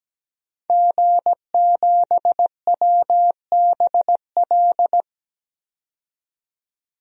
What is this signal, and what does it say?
Morse code, heard as a keyed tone, about 17 wpm, 700 Hz. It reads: G7WBL
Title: DXCC-100 Call Signs - 5 Characters - 17wpm